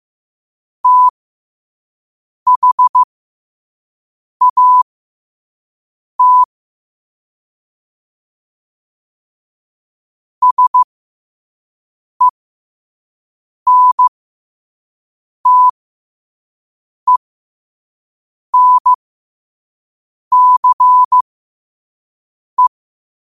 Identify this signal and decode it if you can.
Morse code, heard as a keyed tone, about 15 wpm, spaced as for 5 wpm, 1000 Hz. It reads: THAT SENTENCE
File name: qotdfiles / 5wpm